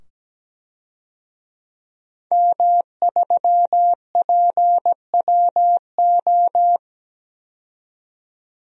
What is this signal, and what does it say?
Morse code, heard as a keyed tone, about 17 wpm, 700 Hz. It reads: M3PWO
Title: DXCC-100 Call Signs - 5 Characters - 17wpm